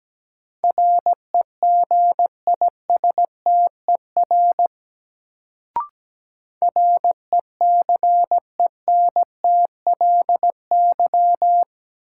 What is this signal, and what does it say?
Morse code, heard as a keyed tone, about 17 wpm, 700 Hz. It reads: REGISTER RECENTLY
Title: Sentences from Top 1200 Words - New Word Drill Alt - 17wpm